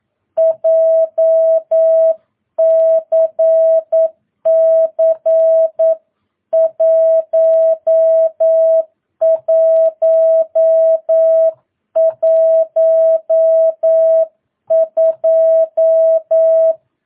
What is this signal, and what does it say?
Morse code, heard as a keyed tone, about 9 wpm, 650 Hz. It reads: JCC 1112
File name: jh8cbh / 2022cw